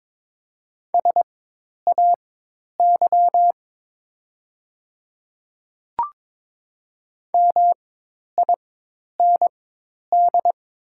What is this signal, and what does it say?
Morse code, heard as a keyed tone, about 22 wpm, spaced as for 10 wpm, 700 Hz. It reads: SAY MIND
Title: 4x Character Spacing - Top 500 Words - 22wpm